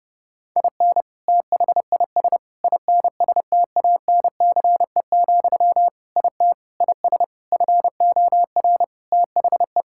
Morse code, heard as a keyed tone, about 30 wpm, 700 Hz. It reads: IN THIS INSTANCE, IT IS FOR THE
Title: Sentences from Top 1000 Words - Review Part 5 - 30wpm